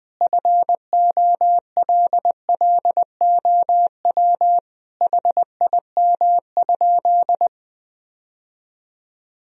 Morse code, heard as a keyed tone, about 20 wpm, 700 Hz. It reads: FOLLOW HIM?
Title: Sentences from Top 1000 Words - Review Part 1 - 20wpm